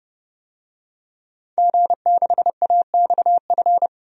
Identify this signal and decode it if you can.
Morse code, heard as a keyed tone, about 30 wpm, 700 Hz. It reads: G6AXF
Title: DXCC-100 Call Signs - 5 Characters - 30wpm